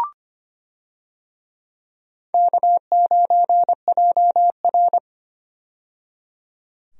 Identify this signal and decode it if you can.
Morse code, heard as a keyed tone, about 25 wpm, 700 Hz. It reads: K9JR
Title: USA Call Signs - 4 Characters - 25wpm